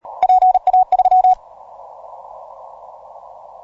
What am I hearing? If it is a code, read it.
Morse code, heard as a keyed tone, about 38 wpm, 740 Hz. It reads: PA3